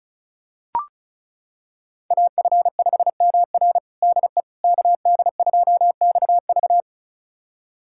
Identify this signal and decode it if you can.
Morse code, heard as a keyed tone, about 36 wpm, 690 Hz. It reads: AF5MR DE KD2XV
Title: USA Callsign DE Callsign - 5 Characters - 35wpm